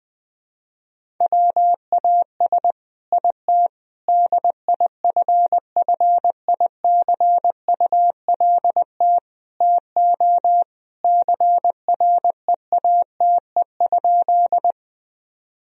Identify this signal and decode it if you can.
Morse code, heard as a keyed tone, about 20 wpm, 700 Hz. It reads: WAS IT DIFFICULT TO CREATE?
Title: Sentences from Top 1000 Words - Review Part 3 - 20wpm